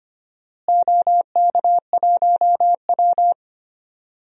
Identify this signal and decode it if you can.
Morse code, heard as a keyed tone, about 25 wpm, 700 Hz. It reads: OK1W
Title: DXCC-100 Call Signs - 4 Characters - Repeated 2x - 25wpm